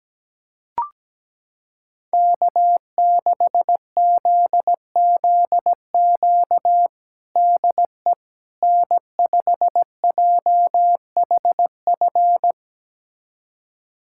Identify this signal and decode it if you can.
Morse code, heard as a keyed tone, about 17 wpm, 700 Hz. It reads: K6ZZQ DE N5JHF
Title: USA Callsign DE Callsign - 5 Characters - 17wpm